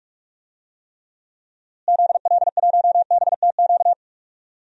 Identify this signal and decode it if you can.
Morse code, heard as a keyed tone, about 42 wpm, 700 Hz. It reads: ZL1BTQ
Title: DXCC-100 Call Signs - 6 Characters - Repeated 2x - 45wpm